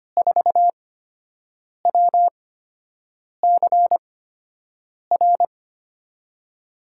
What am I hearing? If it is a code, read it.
Morse code, heard as a keyed tone, about 25 wpm, spaced as for 7 wpm, 700 Hz. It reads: 4WCR
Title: USA Call Signs - 8x Character Spacing - 25wpm